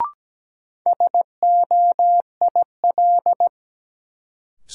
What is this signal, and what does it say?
Morse code, heard as a keyed tone, about 17 wpm, 700 Hz. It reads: SOIL
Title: Sentences from Top 1000 Words - Review Part 4 - New Word Drill - 17wpm